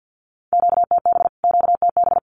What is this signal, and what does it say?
Morse code, heard as a keyed tone, about 50 wpm, 700 Hz. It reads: QTB QTB
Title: Q-Codes Repeated 3x 50wpm